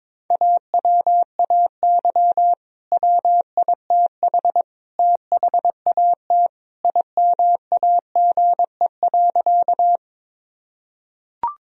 Morse code, heard as a keyed tone, about 22 wpm, 700 Hz. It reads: AWAY WITH THAT IMAGE.
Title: Sentences from Top 1000 Words - Review Part 4 - 22wpm